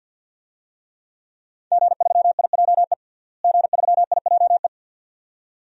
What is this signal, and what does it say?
Morse code, heard as a keyed tone, about 52 wpm, 700 Hz. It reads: G3IJE G3IJE
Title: DXCC-100 Call Signs - 5 Characters - Repeated 2x - 50wpm